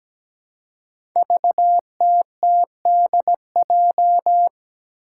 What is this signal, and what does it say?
Morse code, heard as a keyed tone, about 17 wpm, 700 Hz. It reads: VTTDJ
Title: Five Challenging Letter-Number Combinations 17wpm